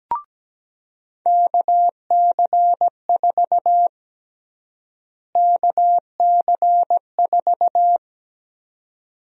Morse code, heard as a keyed tone, about 17 wpm, 700 Hz. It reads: KC4 KC4
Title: USA Call Sign Prefixes - Repeated 2x - 17wpm